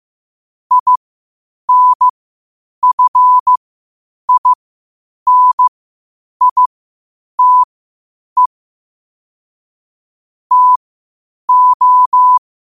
Morse code, heard as a keyed tone, about 15 wpm, spaced as for 8 wpm, 1000 Hz. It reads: INFINITE TO
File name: qotdfiles / 8wpm